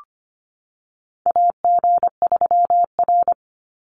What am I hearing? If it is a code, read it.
Morse code, heard as a keyed tone, about 25 wpm, 700 Hz. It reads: AG3R
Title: Call Signs Repeated 3x - 25wpm